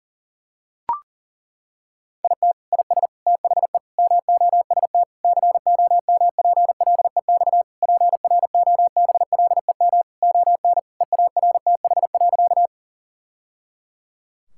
Morse code, heard as a keyed tone, about 40 wpm, 700 Hz. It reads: IT IS THE MOST COMPLEX PROBLEM ON EARTH.
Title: Sentences from Top 1000 Words - Review Part 6 - 40wpm